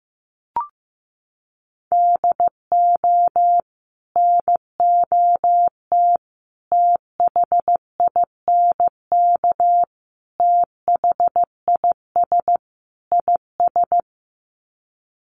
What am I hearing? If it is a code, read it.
Morse code, heard as a keyed tone, about 15 wpm, 700 Hz. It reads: DO NOT THINK THIS IS
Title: Sets of 5 Words Encore2 15wpm